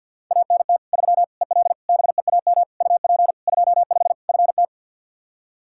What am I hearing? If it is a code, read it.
Morse code, heard as a keyed tone, about 52 wpm, 700 Hz. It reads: ANT 3 EL BEAM UP 25 FT
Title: K1USN SST QSO Practice 50wpm